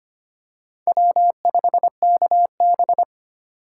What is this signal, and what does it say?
Morse code, heard as a keyed tone, about 25 wpm, 700 Hz. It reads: W5KB
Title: USA Call Signs - 4 Characters - 2x Character Spacing - 25wpm